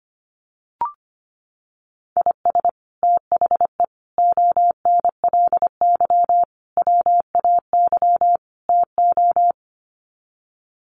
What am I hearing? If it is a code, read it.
Morse code, heard as a keyed tone, about 25 wpm, 700 Hz. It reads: IS THE ONLY WAY TO
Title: Sets of 5 Words Encore 25wpm